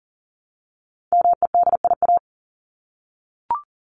Morse code, heard as a keyed tone, about 40 wpm, 700 Hz. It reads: MEDIA
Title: Top 501-750 Words 40wpm